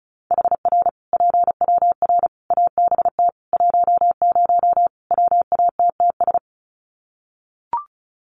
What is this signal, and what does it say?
Morse code, heard as a keyed tone, about 34 wpm, 700 Hz. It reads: HR PWR ABT 10 WATTS
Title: Common QSO Phrases 35wpm